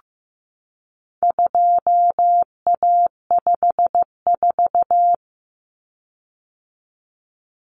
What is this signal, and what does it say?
Morse code, heard as a keyed tone, about 15 wpm, 700 Hz. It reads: 2A54
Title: Common Cut Numbers - 4-Digit 15wpm